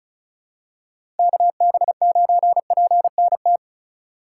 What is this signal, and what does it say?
Morse code, heard as a keyed tone, about 36 wpm, 700 Hz. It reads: KB9PNT
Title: USA Call Signs - 6 Characters - 3x Character Spacing - 35wpm